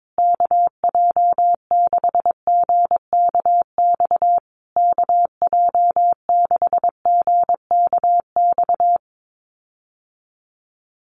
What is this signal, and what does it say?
Morse code, heard as a keyed tone, about 22 wpm, 700 Hz. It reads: KJ6GKX KJ6GKX
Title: USA Call Signs Repeated 3x 22wpm